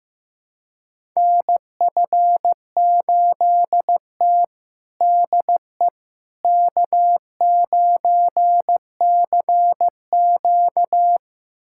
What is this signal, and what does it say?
Morse code, heard as a keyed tone, about 15 wpm, 700 Hz. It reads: NF8T DE K9CQ
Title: USA Callsign DE Callsign - 4 Characters - 15wpm